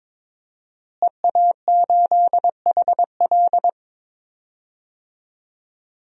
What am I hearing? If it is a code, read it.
Morse code, heard as a keyed tone, about 21 wpm, 700 Hz. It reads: EA8HL
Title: DXCC-100 Call Signs - 5 Characters - Repeated 2x - 22wpm